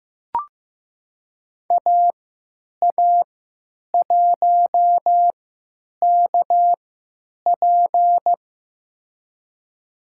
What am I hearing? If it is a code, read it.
Morse code, heard as a keyed tone, about 15 wpm, spaced as for 8 wpm, 700 Hz. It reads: AA1KP
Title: USA Call Signs - 5 Characters - 3x Character Spacing - 15wpm